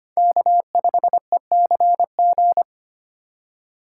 Morse code, heard as a keyed tone, about 25 wpm, 700 Hz. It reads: K5ECG
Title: USA Call Signs - 5 Characters - 4x Character Spacing - 25wpm